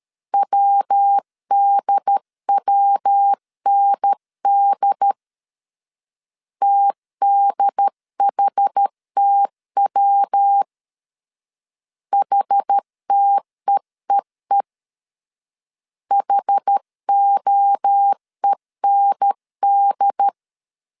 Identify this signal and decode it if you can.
Morse code, heard as a keyed tone, about 13 wpm, 800 Hz. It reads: WDWND TDHTW HTEEE HOEND
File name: CW / DET / 1 3 1500C S A 12wpm 2905